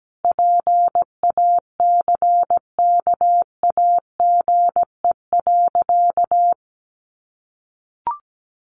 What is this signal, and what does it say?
Morse code, heard as a keyed tone, about 17 wpm, 700 Hz. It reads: PACKAGE.
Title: Additional Easy Phrases 17wpm